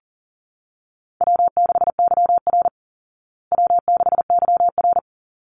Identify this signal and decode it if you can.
Morse code, heard as a keyed tone, about 40 wpm, 700 Hz. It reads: W6YR W6YR
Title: Easy Call Signs Repeated 2x - 40wpm